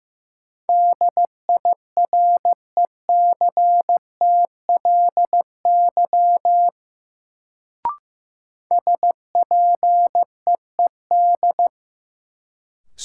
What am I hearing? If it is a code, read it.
Morse code, heard as a keyed tone, about 16 wpm, 700 Hz. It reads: DIRECTLY SPEED
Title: Sentences from Top 800 Words - New Word Drill - 15wpm